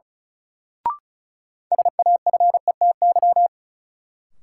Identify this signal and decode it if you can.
Morse code, heard as a keyed tone, about 34 wpm, 700 Hz. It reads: SAFETY
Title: Sentences from Top 700 Words - New Word Drill - 35wpm